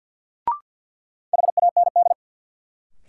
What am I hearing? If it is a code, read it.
Morse code, heard as a keyed tone, about 53 wpm, 700 Hz. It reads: HAND
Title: Sets of 3 Words - New Word Drill - 50wpm